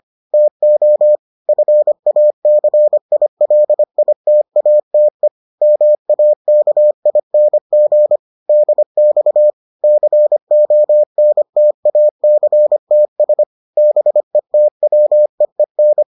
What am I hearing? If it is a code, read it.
Morse code, heard as a keyed tone, about 25 wpm, 600 Hz. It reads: TO FACILITATE MAKING DX CONTACTS BETWEEN